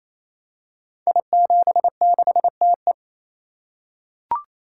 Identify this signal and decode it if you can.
Morse code, heard as a keyed tone, about 28 wpm, 700 Hz. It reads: I76TE
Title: Code Groups - 5 Letter:Numbers - 28wpm